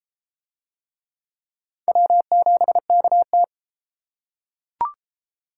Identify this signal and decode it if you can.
Morse code, heard as a keyed tone, about 28 wpm, 700 Hz. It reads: W7KT